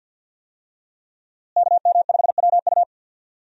Morse code, heard as a keyed tone, about 51 wpm, 700 Hz. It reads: KM5WU
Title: USA Call Signs - 5 Characters - Repeated 2x - 50wpm